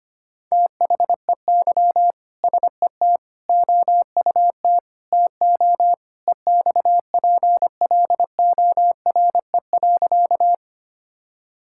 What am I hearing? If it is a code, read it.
Morse code, heard as a keyed tone, about 25 wpm, 700 Hz. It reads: THEY SET OUT TO EXPLORE.
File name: practice / 25wpm / Sentences with 2-Word Phrases - 25wpm